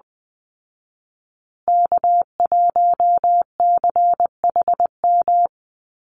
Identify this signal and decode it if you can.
Morse code, heard as a keyed tone, about 19 wpm, 700 Hz. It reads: K1CHM
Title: Call Signs Repeated 2x - 20wpm